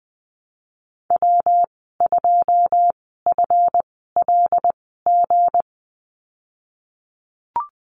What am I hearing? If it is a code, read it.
Morse code, heard as a keyed tone, about 20 wpm, spaced as for 15 wpm, 700 Hz. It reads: W2FLG
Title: 161 Lesson 26g - Letter g - Call Signs 20wpm